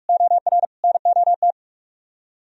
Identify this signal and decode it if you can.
Morse code, heard as a keyed tone, about 48 wpm, 700 Hz. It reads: OR NOT